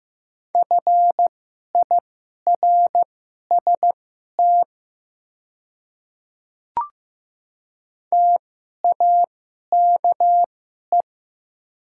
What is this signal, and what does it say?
Morse code, heard as a keyed tone, about 15 wpm, spaced as for 11 wpm, 700 Hz. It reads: FIRST TAKE
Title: 2x Character Spacing - Top 300 Words - 15wpm